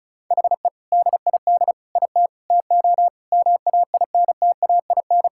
Morse code, heard as a keyed tone, about 36 wpm, 700 Hz. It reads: HE DID IT TO MAINTAIN
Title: Sentences from Top 1000 Words - Review Part 5 - 35wpm